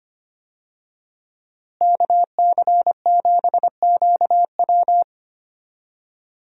Morse code, heard as a keyed tone, about 25 wpm, 700 Hz. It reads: KC7QW
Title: USA Call Signs - 5 Characters - Repeated 3x - 25wpm